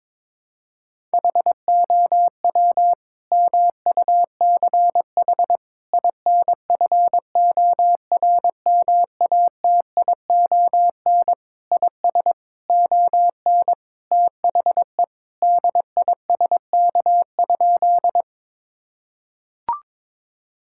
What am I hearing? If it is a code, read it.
Morse code, heard as a keyed tone, about 22 wpm, 700 Hz. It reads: HOW MUCH INFORMATION IS ON THE DISK?
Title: Sentences from Top 1000 Words - Review Part 6 - 22wpm